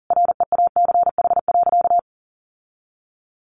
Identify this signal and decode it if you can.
Morse code, heard as a keyed tone, about 40 wpm, 700 Hz. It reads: REACH.
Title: Sentences from Top 1400 Words 40wpm